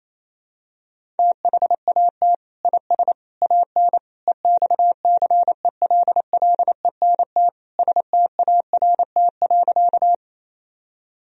Morse code, heard as a keyed tone, about 28 wpm, 700 Hz. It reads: THAT IS AN EXCELLENT START.
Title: Sentences from Top 1000 Words - Review Part 1 - 28wpm